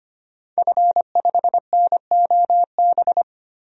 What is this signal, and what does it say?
Morse code, heard as a keyed tone, about 25 wpm, 700 Hz. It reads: F5NOB
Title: DXCC-100 Call Signs - 5 Characters - 25wpm